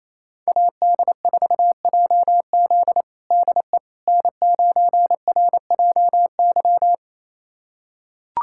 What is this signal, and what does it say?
Morse code, heard as a keyed tone, about 27 wpm, 700 Hz. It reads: AD4JZ DE N9RJY
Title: USA Callsign DE Callsign - 5 Characters - 28wpm